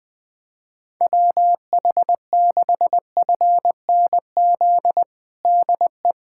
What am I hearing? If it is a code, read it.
Morse code, heard as a keyed tone, about 20 wpm, 700 Hz. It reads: WH6FNZ DE
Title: USA Callsign DE Callsign - 6 Characters - 20wpm